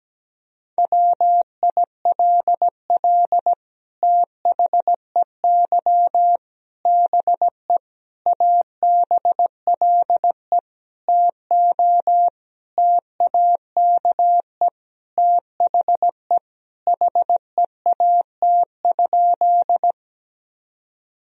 Morse code, heard as a keyed tone, about 17 wpm, 700 Hz. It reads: WILL THEY BE ABLE TO TAKE THE HEAT?
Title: Sentences from Top 1000 Words - Review Part 3 - 17wpm